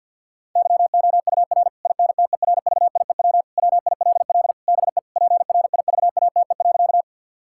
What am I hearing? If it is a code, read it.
Morse code, heard as a keyed tone, about 50 wpm, 700 Hz. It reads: YOUR INTERVIEW WILL BE PRIVATE.
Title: Sentences from Top 1600 Words 50wpm